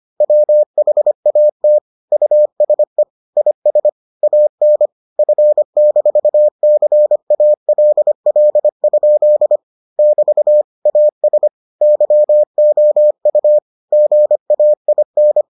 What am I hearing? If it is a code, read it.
Morse code, heard as a keyed tone, about 25 wpm, 600 Hz. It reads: WHAT USE IS AN F-CALL? <BT> AS YOU GAIN